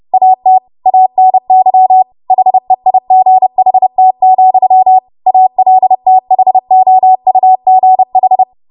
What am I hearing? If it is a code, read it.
Morse code, heard as a keyed tone, about 30 wpm, 750 Hz. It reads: AT ANY HEIGHT, ALTHOUGH
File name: Morse / 30WPM